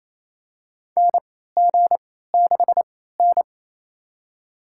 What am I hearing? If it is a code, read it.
Morse code, heard as a keyed tone, about 28 wpm, spaced as for 16 wpm, 700 Hz. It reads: NG6N
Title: USA Call Signs - 4 Characters - 3x Character Spacing - 28wpm